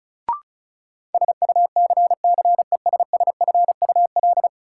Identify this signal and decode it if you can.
Morse code, heard as a keyed tone, about 36 wpm, 700 Hz. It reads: SUCCESSFUL
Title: Sentences from Top 900 Words - New Word Drill - 35wpm